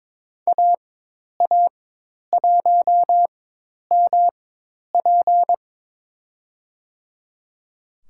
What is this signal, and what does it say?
Morse code, heard as a keyed tone, about 21 wpm, spaced as for 10 wpm, 700 Hz. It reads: AA1MP
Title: USA Call Signs - 5 Characters - 4x Character Spacing - 22wpm